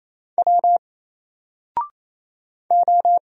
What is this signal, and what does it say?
Morse code, heard as a keyed tone, about 27 wpm, 700 Hz. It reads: W O
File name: Single-Letter-Number-28wpm